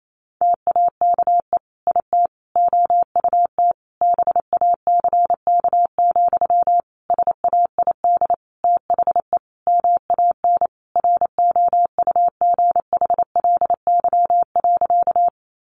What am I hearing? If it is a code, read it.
Morse code, heard as a keyed tone, about 28 wpm, 700 Hz. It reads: TAKE IT OUT BACK, SAID THE MAN ROUGHLY.
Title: Sentences from Top 1500 Words 28wpm